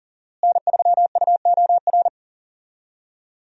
Morse code, heard as a keyed tone, about 40 wpm, 700 Hz. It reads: N3UOR
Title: USA Call Signs - 5 Characters - Repeated 2x - 40wpm